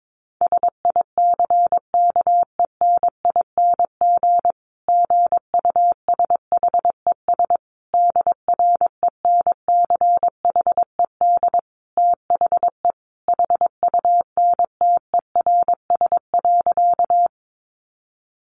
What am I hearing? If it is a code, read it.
Morse code, heard as a keyed tone, about 22 wpm, 700 Hz. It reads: SICKENING GUSHES DRENCHED THE HUNTERS.
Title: A Sound Of Thunder 22wpm